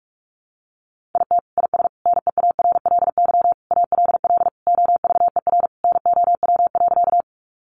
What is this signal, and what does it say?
Morse code, heard as a keyed tone, about 44 wpm, 700 Hz. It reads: IT IS NEARLY ALL OVER NOW.
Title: Sentences from Top 1300 Words 45wpm